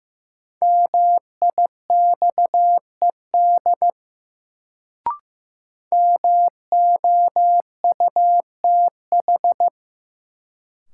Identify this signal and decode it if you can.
Morse code, heard as a keyed tone, about 15 wpm, 700 Hz. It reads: MIXED MOUTH
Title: Sentences from Top 1500 Words - New Word Drill - 15wpm